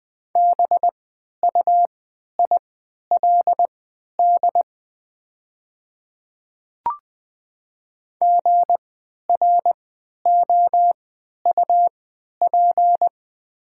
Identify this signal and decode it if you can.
Morse code, heard as a keyed tone, about 20 wpm, spaced as for 11 wpm, 700 Hz. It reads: BUILD GROUP
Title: 3x Character Spacing - Top 500 Words - 20wpm